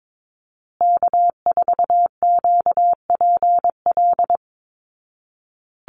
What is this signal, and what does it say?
Morse code, heard as a keyed tone, about 22 wpm, 700 Hz. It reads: K4QPL
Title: Call Signs Repeated 2x - 22wpm